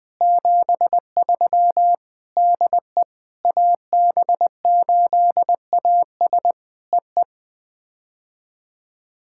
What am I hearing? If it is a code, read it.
Morse code, heard as a keyed tone, about 20 wpm, 700 Hz. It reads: 73 DE AB8AS EE
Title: POTA QSO Practice - 20wpm